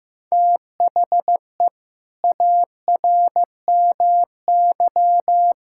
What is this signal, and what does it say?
Morse code, heard as a keyed tone, about 15 wpm, 700 Hz. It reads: THE ARMY
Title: Sentences from Top 1000 Words - Review Part 5 - 15wpm